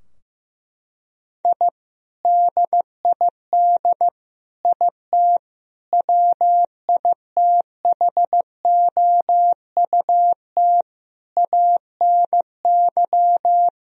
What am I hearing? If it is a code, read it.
Morse code, heard as a keyed tone, about 14 wpm, 700 Hz. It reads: I DID IT WITHOUT ANY
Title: Sentences from Top 1000 Words - Review Part 3 - 15wpm